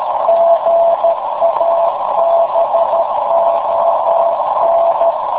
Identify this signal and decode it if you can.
Morse code, heard as a keyed tone, about 13 wpm, 670 Hz, with noise. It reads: GABON
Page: In the afternoon at 1130z TR0A/BCN